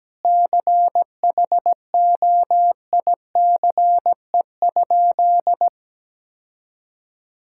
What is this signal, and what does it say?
Morse code, heard as a keyed tone, about 17 wpm, 700 Hz. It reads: CHOICE?
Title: Sentences from Top 1000 Words - Review Part 6 - 17wpm